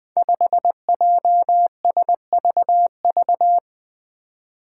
Morse code, heard as a keyed tone, about 20 wpm, 700 Hz. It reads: 5JSVV
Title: Five Challenging Letter-Number Combinations 20wpm